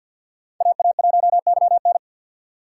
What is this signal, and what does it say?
Morse code, heard as a keyed tone, about 53 wpm, 700 Hz. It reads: AA1YN
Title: USA Call Signs - 5 Characters - 4x Character Spacing - 50wpm